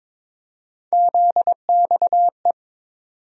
Morse code, heard as a keyed tone, about 22 wpm, 700 Hz. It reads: ZXE
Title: USA Call Sign Postfixes - 3 Letters 22wpm